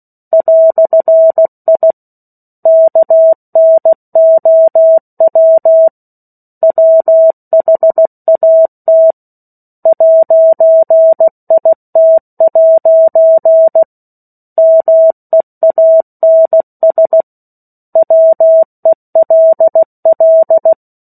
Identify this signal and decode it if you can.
Morse code, heard as a keyed tone, about 16 wpm, 650 Hz. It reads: "I KNOW WHAT 'IT' MEANS WELL
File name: 03-Chapter-03 16 WPM